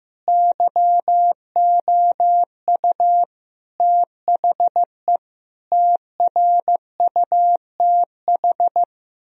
Morse code, heard as a keyed tone, about 15 wpm, 700 Hz. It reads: YOU THE TRUTH
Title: Speed-Racing - Sets of 5 Words 15wpm